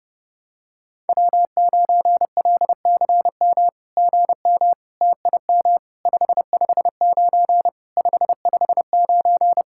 Morse code, heard as a keyed tone, about 30 wpm, 700 Hz. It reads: W9LCM GM TIM 559 559
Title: SOTA QSO Practice 30wpm